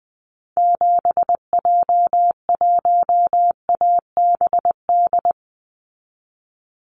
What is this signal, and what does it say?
Morse code, heard as a keyed tone, about 20 wpm, 700 Hz. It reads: 7J1ABD
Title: Call Signs - 20wpm